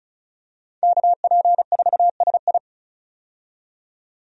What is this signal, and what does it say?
Morse code, heard as a keyed tone, about 34 wpm, 700 Hz. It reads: KP4SI
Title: DXCC-100 Call Signs - 5 Characters - 35wpm